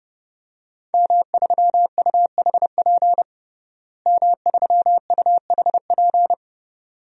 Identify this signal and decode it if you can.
Morse code, heard as a keyed tone, about 30 wpm, 700 Hz. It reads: M3UHP M3UHP
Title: DXCC-100 Call Signs - 5 Characters - Repeated 2x - 30wpm